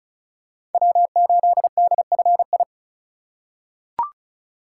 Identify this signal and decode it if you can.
Morse code, heard as a keyed tone, about 36 wpm, 690 Hz. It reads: W8DFI
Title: USA Call Signs - 5 Characters - 2x Character Spacing - 35wpm